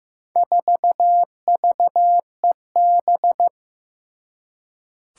Morse code, heard as a keyed tone, about 15 wpm, 700 Hz. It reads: 4VEB
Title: Four Challenging Letter-Number Combinations 15wpm